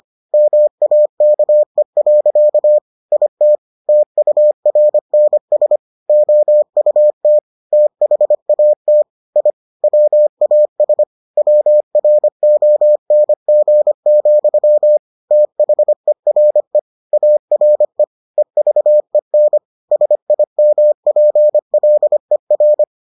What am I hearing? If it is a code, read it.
Morse code, heard as a keyed tone, about 25 wpm, 600 Hz. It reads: MAKE. IT TURNS OUT THAT I WAS WRONG, THERE ARE EVEN SIMPLER